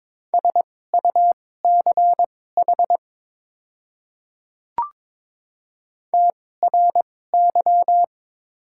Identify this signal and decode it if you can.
Morse code, heard as a keyed tone, about 22 wpm, spaced as for 16 wpm, 700 Hz. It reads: SUCH TRY
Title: 2x Character Spacing - Top 200 Words - 22wpm